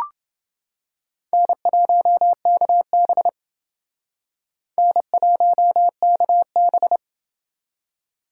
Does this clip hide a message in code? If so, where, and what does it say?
Yes: Morse code from 1.3 to 7.0 s, heard as a keyed tone, about 28 wpm, 700 Hz. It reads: N1KB N1KB